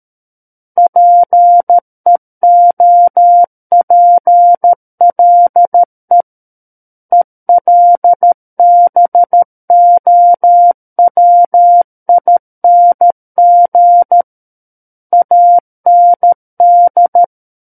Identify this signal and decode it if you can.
Morse code, heard as a keyed tone, about 13 wpm, 700 Hz. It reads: PEOPLE ELBOWING AND
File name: Book One / 04-Chapter 4 - 13 WPM